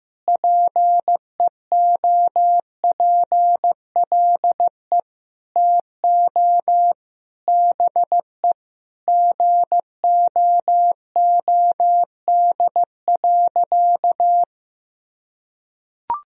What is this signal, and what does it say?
Morse code, heard as a keyed tone, about 15 wpm, 700 Hz. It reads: PEOPLE TO BE GOOD.
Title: Sentences from Top 1000 Words - Review Part 3 - 15wpm